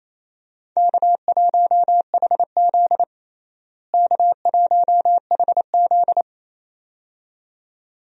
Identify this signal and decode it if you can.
Morse code, heard as a keyed tone, about 28 wpm, 700 Hz. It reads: K1HZ K1HZ
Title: USA Call Signs - 4 Characters - Repeated 2x - 28wpm